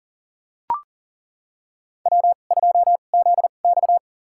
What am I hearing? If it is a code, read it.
Morse code, heard as a keyed tone, about 40 wpm, spaced as for 29 wpm, 700 Hz. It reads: W2ZX
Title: USA Call Signs - 2x Character Spacing - 40wpm